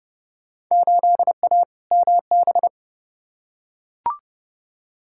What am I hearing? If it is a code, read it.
Morse code, heard as a keyed tone, about 30 wpm, 700 Hz. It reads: 8A MB
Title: Field Day Exchanges Encore - 30wpm